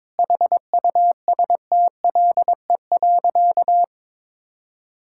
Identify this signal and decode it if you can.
Morse code, heard as a keyed tone, about 22 wpm, 700 Hz. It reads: HUSTLE.
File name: Difficult Sentences with Hints - 22wpm